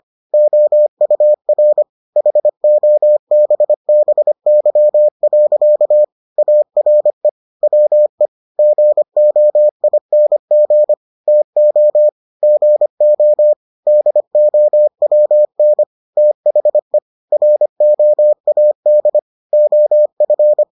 Morse code, heard as a keyed tone, about 25 wpm, 600 Hz. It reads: OUR HOBBY. ARE WE GOING TO GO DOWN THE ROAD OF